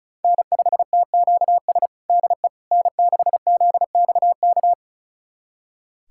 Morse code, heard as a keyed tone, about 36 wpm, 700 Hz. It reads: N5TQS DE N6ZXK
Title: USA Callsign DE Callsign - 5 Characters - 35wpm